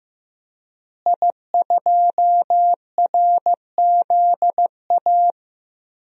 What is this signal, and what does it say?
Morse code, heard as a keyed tone, about 15 wpm, 700 Hz. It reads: I2RZA
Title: DXCC-100 Call Signs - 5 Characters - Repeated 2x - 15wpm